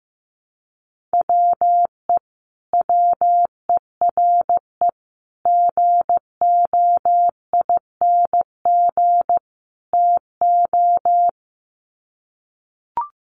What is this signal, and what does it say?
Morse code, heard as a keyed tone, about 15 wpm, 700 Hz. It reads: WE WERE GOING TO
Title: Sets of 4 Words 15wpm